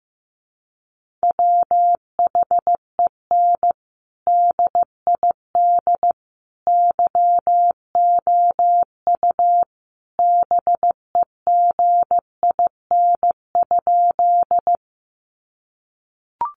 Morse code, heard as a keyed tone, about 15 wpm, 700 Hz. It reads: WHEN DID YOU BEGIN?
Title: Sentences from Top 300 Words 15wpm